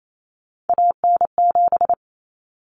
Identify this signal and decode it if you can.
Morse code, heard as a keyed tone, about 28 wpm, 700 Hz. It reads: AN7
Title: Common Cut Numbers - 3-Digit 28wpm